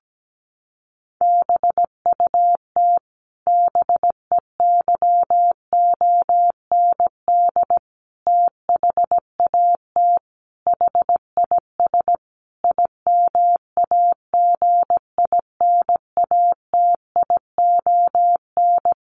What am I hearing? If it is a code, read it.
Morse code, heard as a keyed tone, about 17 wpm, 700 Hz. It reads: BUT BEYOND THAT HIS IMAGINATION
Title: Sentences from Top 1300 Words 17wpm